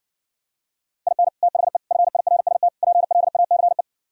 Morse code, heard as a keyed tone, about 69 wpm, 700 Hz. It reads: IN THE FIRST PLACE